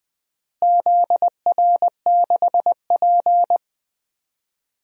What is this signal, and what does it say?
Morse code, heard as a keyed tone, about 19 wpm, 700 Hz. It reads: ZR6P
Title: DXCC-100 Call Signs - 4 Characters - Repeated 3x - 20wpm